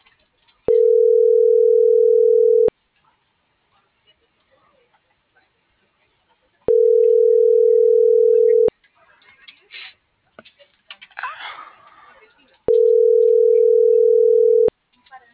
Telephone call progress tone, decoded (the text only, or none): ringback tone